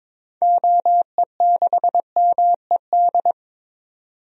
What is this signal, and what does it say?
Morse code, heard as a keyed tone, about 22 wpm, 700 Hz. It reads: OE6MED
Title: DXCC-100 Call Signs - 6 Characters - 22wpm